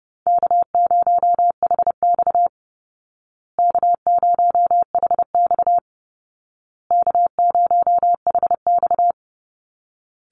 Morse code, heard as a keyed tone, about 30 wpm, 700 Hz. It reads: K0HX K0HX K0HX
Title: Call Signs Repeated 2x - 30wpm